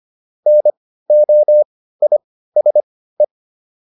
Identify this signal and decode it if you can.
Morse code, heard as a keyed tone, about 25 wpm, spaced as for 15 wpm, 600 Hz. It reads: NOISE